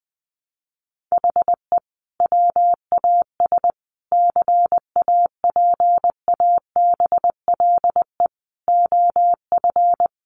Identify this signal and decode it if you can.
Morse code, heard as a keyed tone, about 20 wpm, 700 Hz. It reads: HE WAS CAPABLE OF
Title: Sentences from Top 1200 Words 20wpm